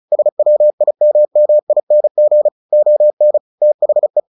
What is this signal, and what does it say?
Morse code, heard as a keyed tone, about 34 wpm, 600 Hz. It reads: SWIMMING ON THE
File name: the old man and the sea 2 35wpm